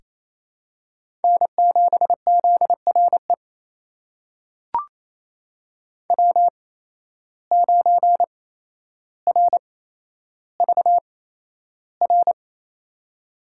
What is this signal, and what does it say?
Morse code, heard as a keyed tone, about 28 wpm, spaced as for 12 wpm, 700 Hz. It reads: N7ZRE W9RVR
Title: USA Call Signs - 5 Characters - 8x Character Spacing - 28wpm